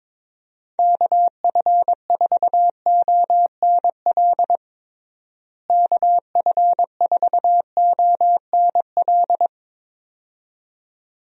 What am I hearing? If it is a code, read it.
Morse code, heard as a keyed tone, about 22 wpm, 700 Hz. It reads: KF4ONL KF4ONL
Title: USA Call Signs - 6 Characters - Repeated 2x - 22wpm